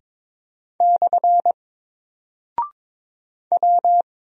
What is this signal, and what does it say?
Morse code, heard as a keyed tone, about 22 wpm, 700 Hz. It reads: / W
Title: Single Character (Letter:Number:Punctuation:Prosign) 22wpm